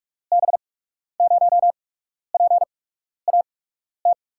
Morse code, heard as a keyed tone, about 46 wpm, spaced as for 12 wpm, 700 Hz. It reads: B0PAT
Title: USA Call Signs - 6 Characters - 8x Character Spacing - 45wpm